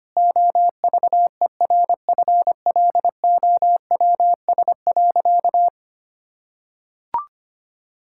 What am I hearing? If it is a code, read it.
Morse code, heard as a keyed tone, about 25 wpm, 700 Hz. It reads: OVERFLOWS.
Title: Pangram Challenge v2 - 25wpm